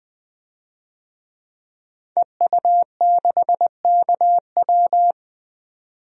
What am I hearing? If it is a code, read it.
Morse code, heard as a keyed tone, about 20 wpm, 700 Hz. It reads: EU6KW
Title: DXCC-100 Call Signs - 5 Characters - 20wpm